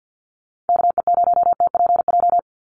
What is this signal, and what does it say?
Morse code, heard as a keyed tone, about 50 wpm, 700 Hz. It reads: KE0TPJ
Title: USA Call Signs 50wpm